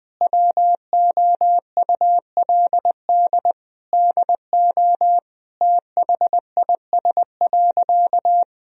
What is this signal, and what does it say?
Morse code, heard as a keyed tone, about 19 wpm, 700 Hz. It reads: WOULD DO THIS.
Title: Sentences from Top 1000 Words - Review Part 3 - 20wpm